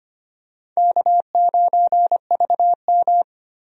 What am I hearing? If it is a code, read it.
Morse code, heard as a keyed tone, about 25 wpm, 700 Hz. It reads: K9VM
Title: USA Call Signs - 4 Characters - Repeated 3x - 25wpm